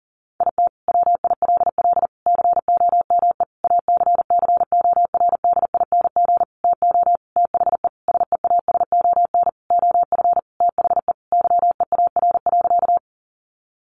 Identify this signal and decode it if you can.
Morse code, heard as a keyed tone, about 40 wpm, 700 Hz. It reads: IT WILL COME ACCORDING TO THE SEASON OF THE YEAR.
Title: Sentences from Top 1400 Words 40wpm